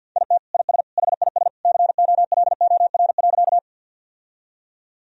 Morse code, heard as a keyed tone, about 53 wpm, 700 Hz. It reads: IT IS HIS COLOR.
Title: Sentences from Top 1000 Words - Review Part 2 - 50wpm